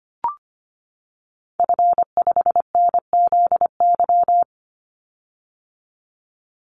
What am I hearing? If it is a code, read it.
Morse code, heard as a keyed tone, about 25 wpm, 700 Hz. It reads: F5NZY
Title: Call Signs - 25wpm